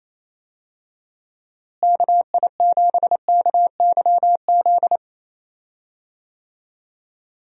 Morse code, heard as a keyed tone, about 28 wpm, 700 Hz. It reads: KI7KYZ
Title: USA Call Signs - 6 Characters - Repeated 2x - 28wpm